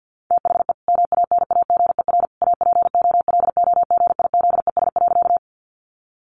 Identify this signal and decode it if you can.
Morse code, heard as a keyed tone, about 50 wpm, 700 Hz. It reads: THE MANAGER APOLOGIZES.
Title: Additional Easy Phrases 50wpm